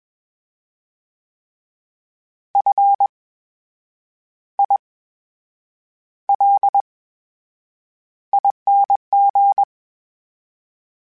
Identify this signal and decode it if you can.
Morse code, heard as a keyed tone, about 21 wpm, spaced as for 8 wpm, 800 Hz. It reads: FILING
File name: Special INGs-mp3s / Special INGs-mp3s / VerySlowToFaster Ing @20